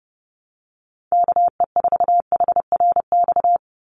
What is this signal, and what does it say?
Morse code, heard as a keyed tone, about 30 wpm, 700 Hz. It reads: KE4HRX
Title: USA Call Signs 30wpm